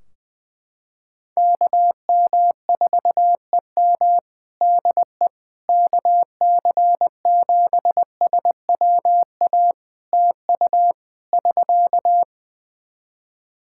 Morse code, heard as a keyed tone, about 20 wpm, 700 Hz. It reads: KM4EM DE KC7SWA TU <SK>
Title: Practice QSO - Exchange 4 - Ending - 20wpm